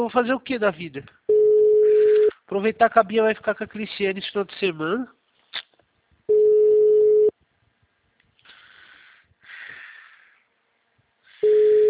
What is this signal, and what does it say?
Ringback tone (Europe, 425 Hz), with speech.
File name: out- -2007- - - .119